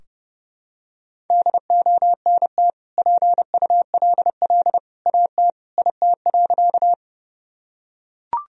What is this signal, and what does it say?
Morse code, heard as a keyed tone, about 30 wpm, 700 Hz. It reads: DONT PULL AT IT.
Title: Sentences from Top 1000 Words - Review Part 2 - 30wpm